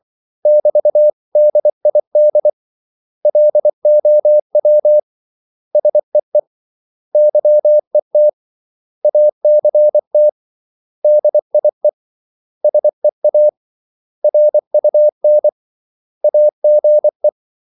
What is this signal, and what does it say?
Morse code, heard as a keyed tone, about 24 wpm, 600 Hz. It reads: <BT>DID LOW SEE YET ACT DIE SEA RUN AGE
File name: CW / Practice / 500MostCommonEnglishWords24